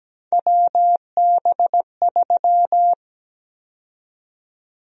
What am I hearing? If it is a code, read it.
Morse code, heard as a keyed tone, about 17 wpm, 700 Hz. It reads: WB3
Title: USA Call Sign Prefixes - 17wpm